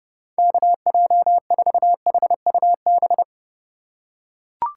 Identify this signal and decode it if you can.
Morse code, heard as a keyed tone, about 32 wpm, 700 Hz. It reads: KJ4HUB
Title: USA Call Signs - 6 Characters - 3x Character Spacing - 30wpm